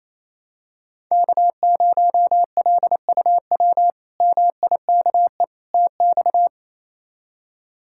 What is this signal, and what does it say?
Morse code, heard as a keyed tone, about 28 wpm, 700 Hz. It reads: K0LUW MIKE TX
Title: K1USN SST QSO Practice - 28wpm